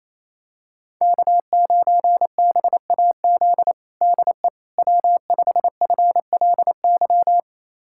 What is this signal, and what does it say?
Morse code, heard as a keyed tone, about 27 wpm, 700 Hz. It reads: K9BAZ DE W5FLY
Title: USA Callsign DE Callsign - 5 Characters - 28wpm